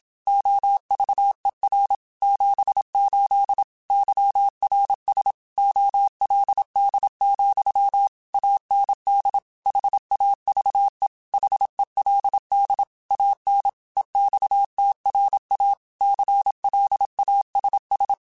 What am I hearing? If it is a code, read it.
Morse code, heard as a keyed tone, about 27 wpm, 800 Hz. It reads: OVER 78 YRS OLD, AND HAVE HELD AN EXTRA CLASS
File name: QSOs / QSOs-mp3s / WA7QOX 2@25